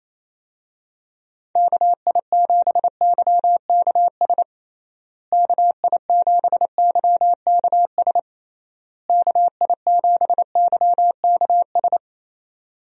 Morse code, heard as a keyed tone, about 28 wpm, 700 Hz. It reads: KI7YKS KI7YKS KI7YKS
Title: USA Call Signs - 6 Characters - Repeated 3x - 28wpm